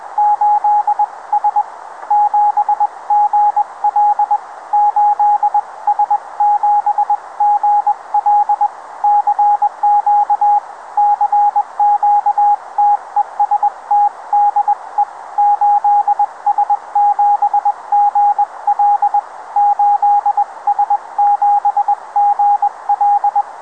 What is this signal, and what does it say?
Morse code, heard as a keyed tone, about 21 wpm, spaced as for 18 wpm, 820 Hz, with radio noise. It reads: 8S7GL 8S7GL CQ CQ TEST DE 8S7GL 8S7GL